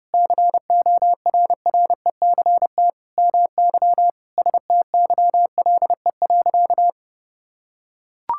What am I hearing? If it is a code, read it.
Morse code, heard as a keyed tone, about 32 wpm, 700 Hz. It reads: CORRECT MY STYLE.
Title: Sentences from Top 1000 Words - Review Part 5 - 30wpm